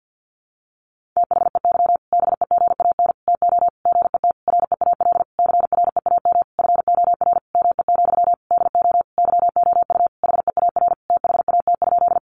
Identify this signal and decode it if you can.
Morse code, heard as a keyed tone, about 50 wpm, 700 Hz. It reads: THEY BEGAN TO GET REAL CREAM FOR ME, DO YOU HEAR THAT?
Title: Sentences from Top 1500 Words 50wpm